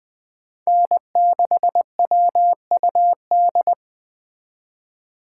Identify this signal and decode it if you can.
Morse code, heard as a keyed tone, about 19 wpm, 700 Hz. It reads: N6WUD